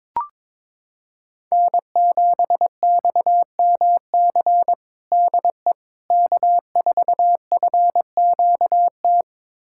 Morse code, heard as a keyed tone, about 21 wpm, 700 Hz. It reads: N7XMC DE K4FQT
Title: USA Callsign DE Callsign 22wpm